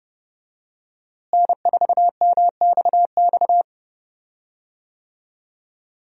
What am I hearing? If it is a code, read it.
Morse code, heard as a keyed tone, about 28 wpm, 700 Hz. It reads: N4MXX
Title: USA Call Signs - 5 Characters - Repeated 3x - 30wpm